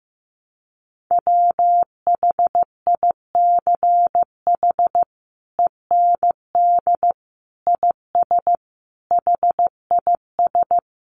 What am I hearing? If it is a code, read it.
Morse code, heard as a keyed tone, about 15 wpm, 700 Hz. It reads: WHICH END IS HIS
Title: Sentences from Top 300 Words 15wpm